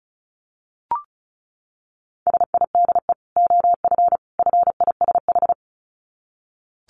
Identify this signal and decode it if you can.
Morse code, heard as a keyed tone, about 34 wpm, 700 Hz. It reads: SIDE OF FISH
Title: Dit Challenge 35wpm